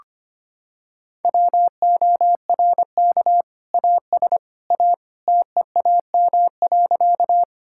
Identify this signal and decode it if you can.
Morse code, heard as a keyed tone, about 25 wpm, 700 Hz. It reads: WORK AS A TEAM.
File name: practice / 25wpm / Sentences with 2-Word Phrases - 25wpm